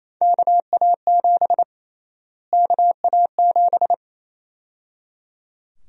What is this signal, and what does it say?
Morse code, heard as a keyed tone, about 27 wpm, 700 Hz. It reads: KA7 KA7
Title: USA Call Sign Prefixes - Repeated 2x - 28wpm